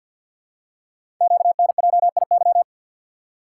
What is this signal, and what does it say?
Morse code, heard as a keyed tone, about 50 wpm, 690 Hz. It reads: QNJIY
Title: Code Groups - 5 Letters - 50wpm